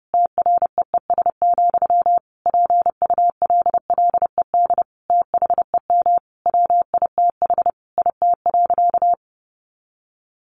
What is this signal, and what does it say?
Morse code, heard as a keyed tone, about 30 wpm, 700 Hz. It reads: TREES, PULLED THEM WITH IT.
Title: A Sound Of Thunder 30wpm